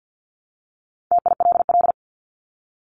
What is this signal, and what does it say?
Morse code, heard as a keyed tone, about 50 wpm, 700 Hz. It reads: TILL
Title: Top 1751-2000 Words 50wpm